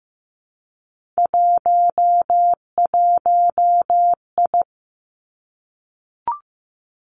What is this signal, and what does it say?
Morse code, heard as a keyed tone, about 15 wpm, 700 Hz. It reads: 11I
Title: Three Challenging Letter-Number Combinations 15wpm